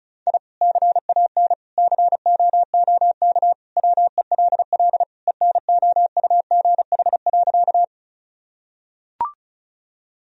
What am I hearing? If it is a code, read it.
Morse code, heard as a keyed tone, about 36 wpm, 700 Hz. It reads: I CAN COOK WELL ENOUGH.
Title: Sentences from Top 1000 Words - Review Part 4 - 35wpm